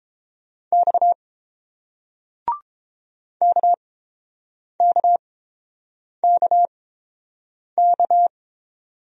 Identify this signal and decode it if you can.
Morse code, heard as a keyed tone, about 30 wpm, spaced as for 7 wpm, 700 Hz. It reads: X KKKK